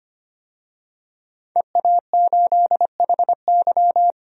Morse code, heard as a keyed tone, about 25 wpm, 700 Hz. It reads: EA8HY
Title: DXCC-100 Call Signs - 5 Characters - 25wpm